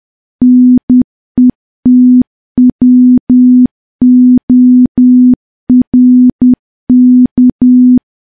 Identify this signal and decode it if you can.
Morse code, heard as a keyed tone, about 10 wpm, 250 Hz. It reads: NETWORK
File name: morse / morse-training-10WPM